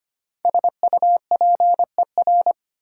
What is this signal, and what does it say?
Morse code, heard as a keyed tone, about 25 wpm, 700 Hz. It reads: SUPER
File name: 5 Letter Words - First 3 Letters Given - 25wpm